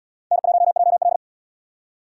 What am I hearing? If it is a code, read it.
Morse code, heard as a keyed tone, about 64 wpm, 700 Hz. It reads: N1JL